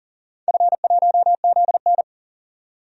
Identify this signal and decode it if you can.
Morse code, heard as a keyed tone, about 40 wpm, 700 Hz. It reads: F1ZN
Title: DXCC-100 Call Signs - 4 Characters - 40wpm